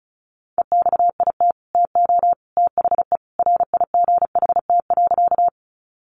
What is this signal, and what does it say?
Morse code, heard as a keyed tone, about 34 wpm, 700 Hz. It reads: EXIT TO THE RIGHT.
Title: Sentences from Top 1100 Words 35wpm